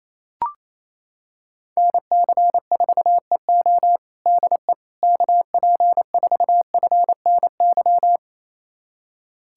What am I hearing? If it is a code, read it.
Morse code, heard as a keyed tone, about 28 wpm, 700 Hz. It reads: NC4EO DE KP4FNY
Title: USA Callsign DE Callsign 28wpm